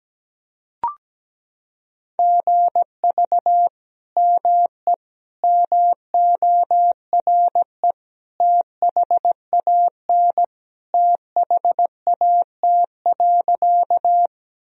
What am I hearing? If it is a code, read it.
Morse code, heard as a keyed tone, about 17 wpm, 700 Hz. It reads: GV ME MORE THAN THAT.
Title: Sentences from Top 200 Words and Abbreviations 17wpm